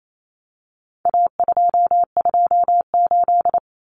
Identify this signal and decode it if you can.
Morse code, heard as a keyed tone, about 28 wpm, 700 Hz. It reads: A228
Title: Common Cut Numbers - 4-Digit 28wpm